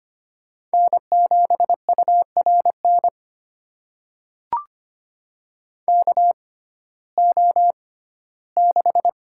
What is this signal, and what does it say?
Morse code, heard as a keyed tone, about 25 wpm, spaced as for 15 wpm, 700 Hz. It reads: N7URN KO6
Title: USA Call Signs - 6x Character Spacing - 25wpm